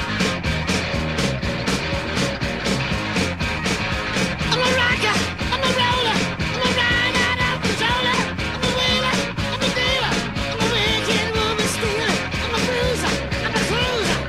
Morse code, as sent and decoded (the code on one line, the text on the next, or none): none